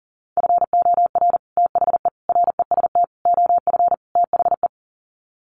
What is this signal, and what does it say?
Morse code, heard as a keyed tone, about 37 wpm, 700 Hz. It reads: FOR THE REST OF THE
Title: Sets of 5 Words 40wpm